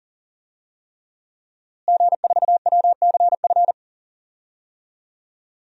Morse code, heard as a keyed tone, about 40 wpm, 700 Hz. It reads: G4WCF
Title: DXCC-100 Call Signs - 5 Characters - Repeated 3x - 40wpm